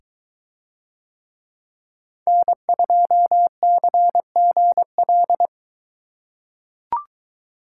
Morse code, heard as a keyed tone, about 22 wpm, 700 Hz. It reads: N2CGL